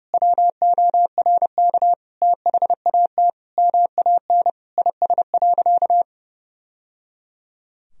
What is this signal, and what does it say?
Morse code, heard as a keyed tone, about 30 wpm, 700 Hz. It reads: WORK THAT MAN IS.
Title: Sentences from Top 1000 Words - Review Part 2 - 30wpm